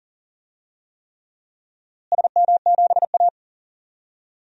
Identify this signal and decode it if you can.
Morse code, heard as a keyed tone, about 40 wpm, 700 Hz. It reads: SM7A
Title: DXCC-100 Call Signs - 4 Characters - Repeated 3x - 40wpm